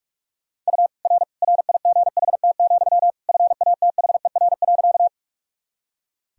Morse code, heard as a keyed tone, about 45 wpm, 700 Hz. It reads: U R RIGHT, FATHER.
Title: Sentences from Top 300 Words and Abbreviations 45wpm